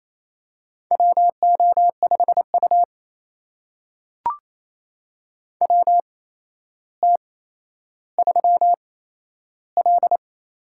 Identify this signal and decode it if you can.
Morse code, heard as a keyed tone, about 28 wpm, spaced as for 12 wpm, 700 Hz. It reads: WO5U WT3L
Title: USA Call Signs - 4 Characters - 8x Character Spacing - 28wpm